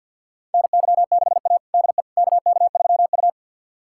Morse code, heard as a keyed tone, about 52 wpm, 700 Hz. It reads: NY6A DE KK3U
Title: USA Callsign DE Callsign - 4 Characters - 50wpm